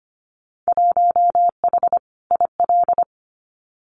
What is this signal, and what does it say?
Morse code, heard as a keyed tone, about 25 wpm, 700 Hz. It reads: 1H IL
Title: Winter Field Day Exchanges 25wpm